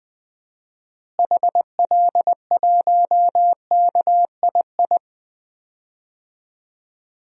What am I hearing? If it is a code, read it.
Morse code, heard as a keyed tone, about 20 wpm, 700 Hz. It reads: HL1KII
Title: DXCC-100 Call Signs - 6 Characters - Repeated 2x - 20wpm